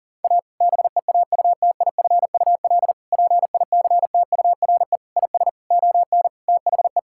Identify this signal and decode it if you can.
Morse code, heard as a keyed tone, about 37 wpm, 700 Hz. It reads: A BEAUTIFUL PICTURE IS ON THE
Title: Sentences from Top 1000 Words - Review Part 6 - 40wpm